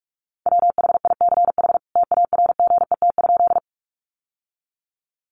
Morse code, heard as a keyed tone, about 44 wpm, 700 Hz. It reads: WHICH TARGET?
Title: Sentences from Top 1200 Words 45wpm